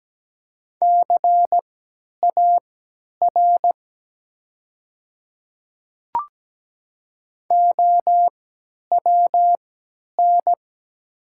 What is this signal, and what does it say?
Morse code, heard as a keyed tone, about 17 wpm, spaced as for 10 wpm, 700 Hz. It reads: CAR OWN
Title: 3x Character Spacing - Top 500 Words - 17wpm